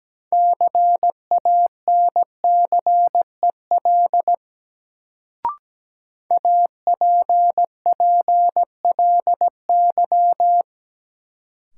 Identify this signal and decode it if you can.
Morse code, heard as a keyed tone, about 17 wpm, 700 Hz. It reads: CANCEL APPLY
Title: Sentences from Top 1200 Words - New Word Drill Alt - 17wpm